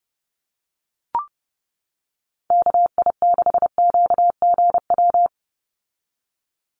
Morse code, heard as a keyed tone, about 30 wpm, 700 Hz. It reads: KI6QGW
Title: USA Call Signs 30wpm